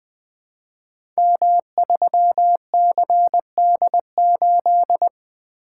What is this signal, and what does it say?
Morse code, heard as a keyed tone, about 20 wpm, 700 Hz. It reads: M3CD8
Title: Code Groups - 5 Letter:Numbers - 20wpm